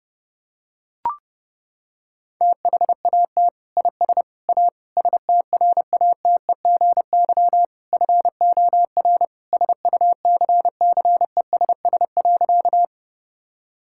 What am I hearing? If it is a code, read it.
Morse code, heard as a keyed tone, about 30 wpm, 700 Hz. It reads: THAT IS A STRATEGY FOR SUCCESS.
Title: Sentences from Top 1000 Words - Review Part 5 - 30wpm